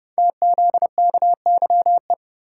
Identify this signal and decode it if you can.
Morse code, heard as a keyed tone, about 30 wpm, 700 Hz. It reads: TZKYE
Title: Code Groups - 5 Letters - 30wpm